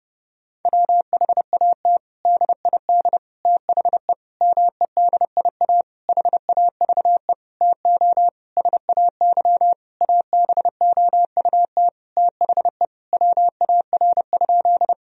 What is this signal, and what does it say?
Morse code, heard as a keyed tone, about 30 wpm, 700 Hz. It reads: WHAT DID THE MEDIA HAVE TO SAY ABOUT THE WAR?
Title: Sentences from Top 1000 Words - Review Part 4 - 30wpm